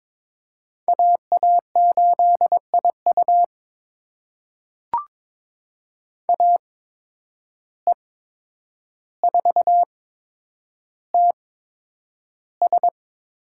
Morse code, heard as a keyed tone, about 21 wpm, spaced as for 9 wpm, 700 Hz. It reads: AA8IU AE4TS
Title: USA Call Signs - 5 Characters - 8x Character Spacing - 22wpm